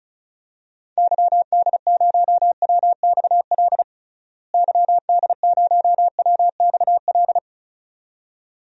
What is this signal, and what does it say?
Morse code, heard as a keyed tone, about 34 wpm, 700 Hz. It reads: YD0WXL YD0WXL
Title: DXCC-100 Call Signs - 6 Characters - Repeated 2x - 35wpm